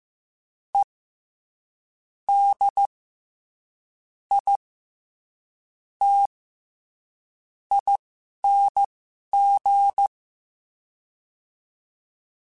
Morse code, heard as a keyed tone, about 15 wpm, spaced as for 11 wpm, 800 Hz. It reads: E D I T ING
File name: Special INGs-mp3s / Special INGs-mp3s / SlowToFaster ing @10